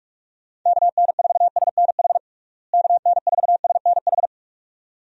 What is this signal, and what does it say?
Morse code, heard as a keyed tone, about 45 wpm, 700 Hz. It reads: KN4SNH KN4SNH
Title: USA Call Signs - 6 Characters - Repeated 2x - 45wpm